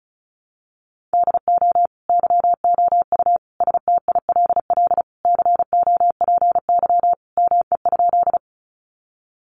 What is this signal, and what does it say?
Morse code, heard as a keyed tone, about 34 wpm, 700 Hz. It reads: DO YOU STILL COPY ME?
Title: Sentences from Top 1400 Words 35wpm